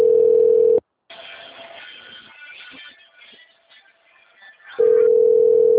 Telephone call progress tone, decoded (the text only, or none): ringback tone